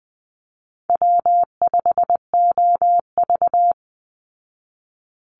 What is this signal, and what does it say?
Morse code, heard as a keyed tone, about 20 wpm, 700 Hz. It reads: W5OV
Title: Easy Call Signs Repeated 3x - 20wpm